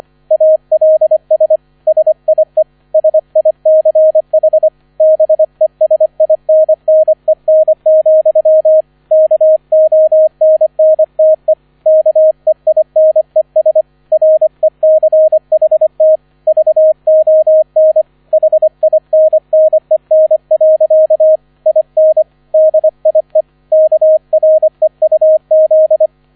Morse code, heard as a keyed tone, about 25 wpm, 630 Hz, with radio noise. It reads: ALS SIE SICH BESINNEN, KONNTE KEINES RECHT VON HINNEN. IN DIE KREUZ